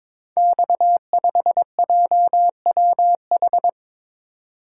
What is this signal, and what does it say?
Morse code, heard as a keyed tone, about 22 wpm, 700 Hz. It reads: X5JWH
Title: Code Groups - 5 Letter:Numbers - 22wpm